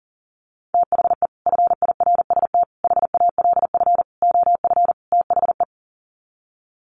Morse code, heard as a keyed tone, about 40 wpm, 700 Hz. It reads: THE FIRST HALF OF THE
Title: Sets of 5 Words 40wpm